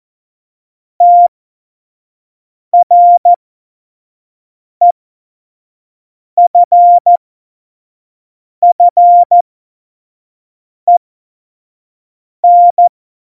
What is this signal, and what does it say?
Morse code, heard as a keyed tone, about 14 wpm, spaced as for 5 wpm, 700 Hz. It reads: TREFFEN
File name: Morsekurs / Witze / Witz 31.Juli 2018, 25 F70